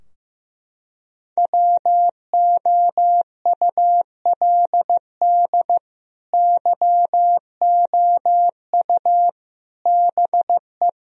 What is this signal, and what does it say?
Morse code, heard as a keyed tone, about 16 wpm, 700 Hz. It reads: WOULD YOU BE
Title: Sentences from Top 1000 Words - Review Part 2 - 15wpm